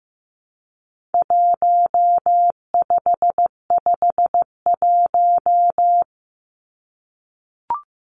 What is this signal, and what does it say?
Morse code, heard as a keyed tone, about 15 wpm, 700 Hz. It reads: 1551
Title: Numbers - Contest Style - 4-Digit 15wpm